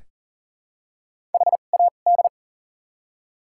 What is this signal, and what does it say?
Morse code, heard as a keyed tone, about 39 wpm, spaced as for 29 wpm, 700 Hz. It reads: HAD
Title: 2x Character Spacing - Top 300 Words - 40wpm